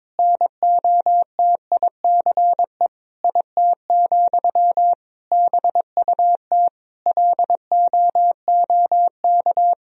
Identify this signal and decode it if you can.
Morse code, heard as a keyed tone, about 22 wpm, 700 Hz. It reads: NOTICE IT, BUT LOOK
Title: Sentences from Top 1000 Words - Review Part 2 - 22wpm